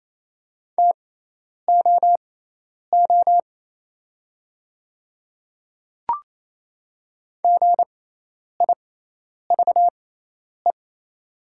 Morse code, heard as a keyed tone, about 27 wpm, spaced as for 10 wpm, 700 Hz. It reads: TOO GIVE
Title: 6x Character Spacing - Top 200 Words - 28wpm